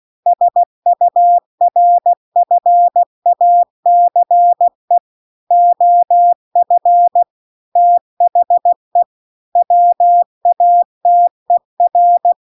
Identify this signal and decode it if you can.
Morse code, heard as a keyed tone, about 16 wpm, 700 Hz. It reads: SURFACE OF THE WATER
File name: CW-The-Big-Twoheaded-River-1-16WPM